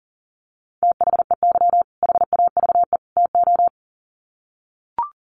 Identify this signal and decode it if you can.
Morse code, heard as a keyed tone, about 40 wpm, 700 Hz. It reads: THEY HAVE TO
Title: Sets of 3 Words 40wpm